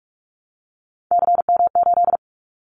Morse code, heard as a keyed tone, about 44 wpm, 700 Hz. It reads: CM8
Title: Callsign Prefixes for Top 100 DXCC - 45wpm